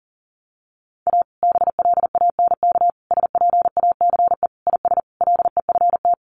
Morse code, heard as a keyed tone, about 40 wpm, 700 Hz. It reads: A BLANK SPACE IS LEFT
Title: Sentences from Top 1500 Words 40wpm